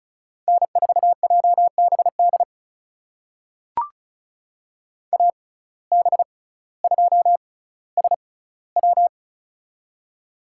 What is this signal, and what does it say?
Morse code, heard as a keyed tone, about 34 wpm, spaced as for 18 wpm, 700 Hz. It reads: N4JBD AB2SW
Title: USA Call Signs - 5 Characters - 6x Character Spacing - 35wpm